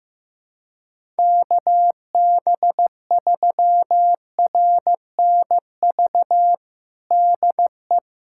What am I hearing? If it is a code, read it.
Morse code, heard as a keyed tone, about 15 wpm, 700 Hz. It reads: KB3RNV DE
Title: USA Callsign DE Callsign 15wpm